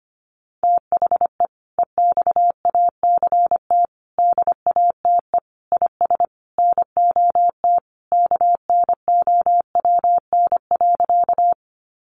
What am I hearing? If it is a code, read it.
Morse code, heard as a keyed tone, about 25 wpm, 700 Hz. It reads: THE EXACT DATE IS NOT KNOWN.
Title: Sentences from Top 1300 Words 25wpm